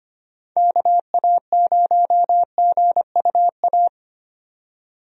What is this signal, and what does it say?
Morse code, heard as a keyed tone, about 25 wpm, 700 Hz. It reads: KA0GUA
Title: USA Call Signs - 6 Characters - 25wpm